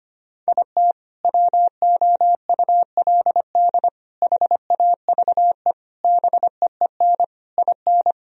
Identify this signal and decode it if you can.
Morse code, heard as a keyed tone, about 25 wpm, 700 Hz. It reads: IT WOULD HAVE BEEN IN
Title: Sentences from Top 1000 Words - Review Part 1 - 25wpm